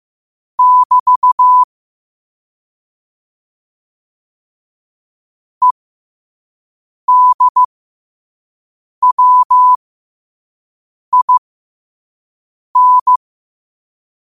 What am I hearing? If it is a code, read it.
Morse code, heard as a keyed tone, about 15 wpm, spaced as for 5 wpm, 1000 Hz. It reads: <BT> EDWIN
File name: qotdfiles / 5wpm